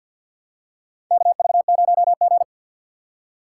Morse code, heard as a keyed tone, about 51 wpm, 700 Hz. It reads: KV0G
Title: USA Call Signs - 4 Characters - 6x Character Spacing - 50wpm